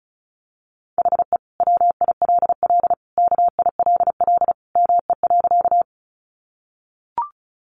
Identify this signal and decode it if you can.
Morse code, heard as a keyed tone, about 34 wpm, 700 Hz. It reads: HE WILL KILL ME.
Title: Sentences from Top 1400 Words 35wpm